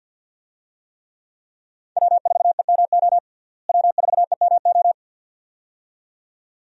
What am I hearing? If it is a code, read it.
Morse code, heard as a keyed tone, about 53 wpm, 700 Hz. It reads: W4EMO W4EMO
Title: USA Call Signs - 5 Characters - Repeated 2x - 50wpm